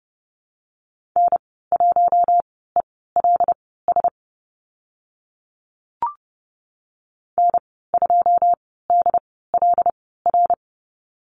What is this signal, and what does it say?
Morse code, heard as a keyed tone, about 30 wpm, spaced as for 17 wpm, 700 Hz. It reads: N1ELS N2DLR
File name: Lesson 14g - Number 2 - Call Signs 30wpm